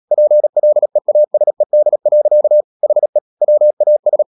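Morse code, heard as a keyed tone, about 37 wpm, 600 Hz. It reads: PLEASED. HE WAS
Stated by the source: Morse code practice at 35–39 wpm sending English text of Ken Follett's novel Never, from CW med Gnister.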